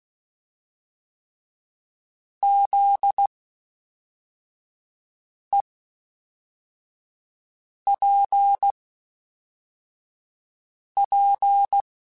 Morse code, heard as a keyed tone, about 16 wpm, spaced as for 4 wpm, 800 Hz. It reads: ZEPP